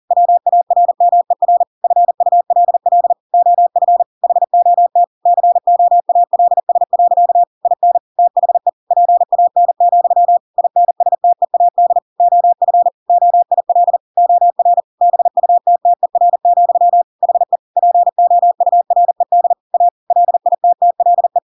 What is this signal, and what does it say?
Morse code, heard as a keyed tone, about 40 wpm, 700 Hz. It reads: WARMER FULL OF HOT COALS. IN THE PAN, INSTEAD OF OIL OR BUTTER, HE POURED A LITTLE
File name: Pinocchio / Pinocchio ch5 40 40 ik1hge